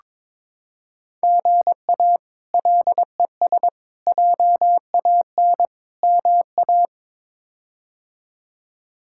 Morse code, heard as a keyed tone, about 22 wpm, 700 Hz. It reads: GA LES JAN MA
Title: K1USN SST QSO Practice - 22wpm